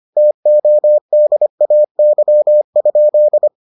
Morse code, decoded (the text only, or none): TODAY?